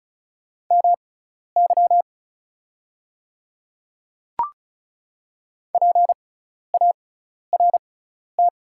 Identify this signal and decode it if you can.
Morse code, heard as a keyed tone, about 36 wpm, spaced as for 12 wpm, 690 Hz. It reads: MY PART
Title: 6x Character Spacing - Top 300 Words - 35wpm